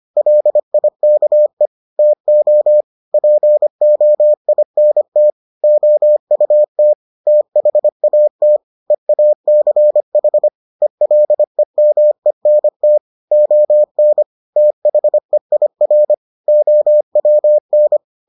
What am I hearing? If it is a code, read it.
Morse code, heard as a keyed tone, about 25 wpm, 600 Hz. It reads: LIKE TO POINT OUT THAT EACH ELEMENT ON THEIR OWN